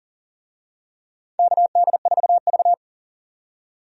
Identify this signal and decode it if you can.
Morse code, heard as a keyed tone, about 40 wpm, 700 Hz. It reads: KD4V
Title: USA Call Signs - 4 Characters - 40wpm